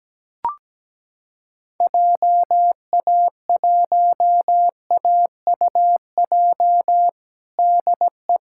Morse code, heard as a keyed tone, about 17 wpm, 700 Hz. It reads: JA1AUJ DE
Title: USA Callsign DE Callsign 17wpm